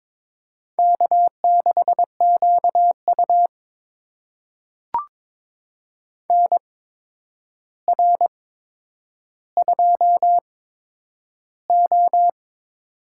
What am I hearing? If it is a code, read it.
Morse code, heard as a keyed tone, about 22 wpm, spaced as for 9 wpm, 700 Hz. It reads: K6QU NR2O
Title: USA Call Signs - 4 Characters - 8x Character Spacing - 22wpm